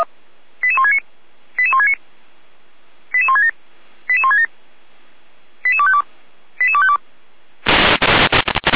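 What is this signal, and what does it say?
Five-tone selective call (ZVEI), heard as quick runs of five tones, tones of about 70 ms. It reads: 88169 88169 88167 88167 88252 88252
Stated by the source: ZVEI five-tone selective call received by SDR, associated with Feuerwehr Orlamünde fire brigade.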